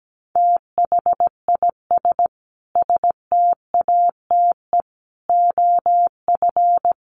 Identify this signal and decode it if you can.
Morse code, heard as a keyed tone, about 17 wpm, 700 Hz. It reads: THIS STATE OF
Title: Sentences from Top 1100 Words 17wpm